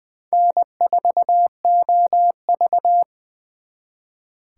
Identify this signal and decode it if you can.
Morse code, heard as a keyed tone, about 25 wpm, 700 Hz. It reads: N4OV
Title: Speed Racing - USA Call Signs 20wpm